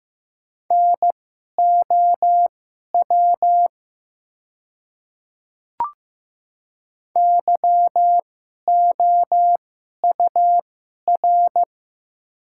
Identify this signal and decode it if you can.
Morse code, heard as a keyed tone, about 15 wpm, spaced as for 11 wpm, 700 Hz. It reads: NOW YOUR
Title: 2x Character Spacing - Top 100 Words - 15wpm